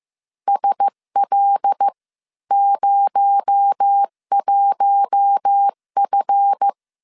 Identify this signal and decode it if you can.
Morse code, heard as a keyed tone, about 15 wpm, spaced as for 12 wpm, 800 Hz. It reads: SL01F
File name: CW / DET / 1 9 1500C S B 14wpm 3302